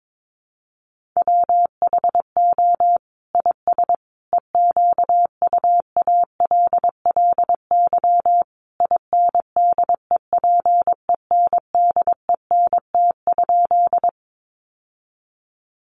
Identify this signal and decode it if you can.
Morse code, heard as a keyed tone, about 22 wpm, 700 Hz. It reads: WHO IS EQUALLY INDEPENDENT?
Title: Sentences from Top 1400 Words 22wpm